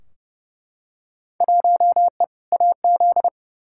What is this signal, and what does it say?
Morse code, heard as a keyed tone, about 30 wpm, 700 Hz. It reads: 1E AZ
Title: Field Day Exchanges - 30wpm